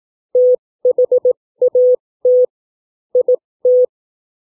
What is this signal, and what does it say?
Morse code, heard as a keyed tone, about 18 wpm, spaced as for 15 wpm, 500 Hz. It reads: THAT IT
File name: CW / book reading 3